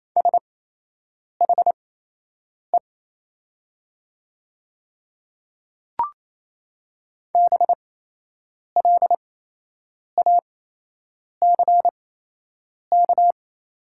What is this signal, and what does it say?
Morse code, heard as a keyed tone, about 28 wpm, spaced as for 8 wpm, 700 Hz. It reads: SHE BLACK
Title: 8x Character Spacing - Top 500 Words - 28wpm